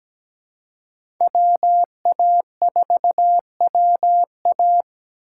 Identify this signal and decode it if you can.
Morse code, heard as a keyed tone, about 17 wpm, 700 Hz. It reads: WA4WA
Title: USA Call Signs - 5 Characters - 2x Character Spacing - 17wpm